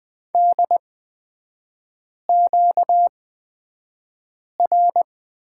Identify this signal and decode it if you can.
Morse code, heard as a keyed tone, about 20 wpm, spaced as for 5 wpm, 700 Hz. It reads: DQR
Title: 236 Lesson 38e - Letter q - Three Characters 20wpm